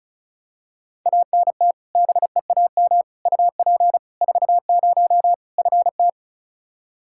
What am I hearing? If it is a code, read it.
Morse code, heard as a keyed tone, about 36 wpm, 700 Hz. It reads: ANT BEAM UP 40 FT
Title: Practice QSO - Exchange 2 - Info - 35wpm